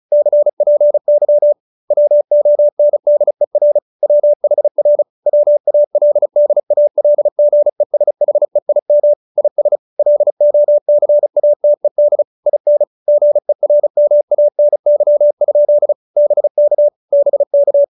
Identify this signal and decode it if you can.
Morse code, heard as a keyed tone, about 35 wpm, 590 Hz. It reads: CPY WONDER WHR WALDALGESHEIM IS LOCATED IN GERMANY? BK BK